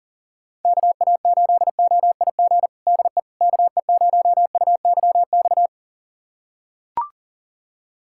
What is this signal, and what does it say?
Morse code, heard as a keyed tone, about 37 wpm, 700 Hz. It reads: KA8OIG DE KE0UYX
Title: USA Callsign DE Callsign - 6 Characters - 40wpm